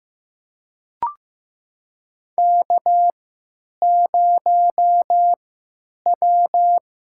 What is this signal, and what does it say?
Morse code, heard as a keyed tone, about 15 wpm, spaced as for 9 wpm, 700 Hz. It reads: K0W
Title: USA Call Signs - 5 Characters - 3x Character Spacing - 15wpm